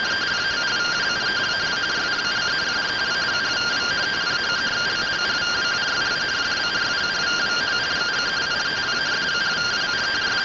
Two-tone teleprinter signal, about 170 Hz shift.